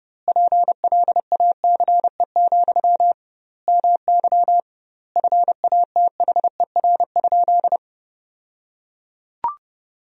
Morse code, heard as a keyed tone, about 30 wpm, 700 Hz. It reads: PLACE, MY FATHER?
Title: 2x Word-Spacing - Sentences from Top 400 Words - 30wpm